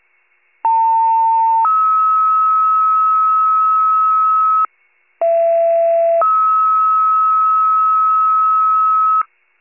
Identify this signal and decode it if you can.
Two-tone page, heard as a one-second tone, then a three-second tone, about 900 then 1290 Hz, about 670 then 1290 Hz. It is heard from 0.6 to 9.2 s.